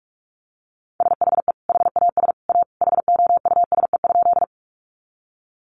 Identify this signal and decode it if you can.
Morse code, heard as a keyed tone, about 44 wpm, 700 Hz. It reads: SHE HAS A HOUSE?
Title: Sentences from Top 200 Words 45wpm